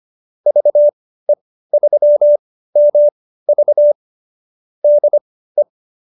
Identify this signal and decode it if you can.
Morse code, heard as a keyed tone, about 25 wpm, spaced as for 15 wpm, 590 Hz. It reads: VE3MV DE